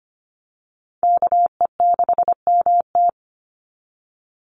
Morse code, heard as a keyed tone, about 25 wpm, 700 Hz. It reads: KE6MT
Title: NA SOTA Chaser Callsigns 25wpm